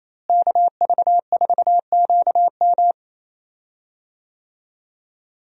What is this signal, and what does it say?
Morse code, heard as a keyed tone, about 28 wpm, 700 Hz. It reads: KV4QM
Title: USA Call Signs - 5 Characters - Repeated 2x - 28wpm